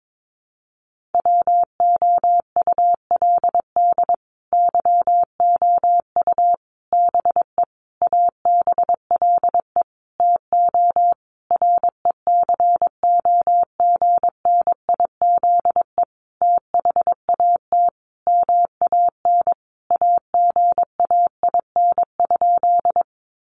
Morse code, heard as a keyed tone, about 22 wpm, 700 Hz. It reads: WOULD YOU BE ABLE TO RECOGNIZE THAT MAN AGAIN?
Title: Sentences from Top 1400 Words 22wpm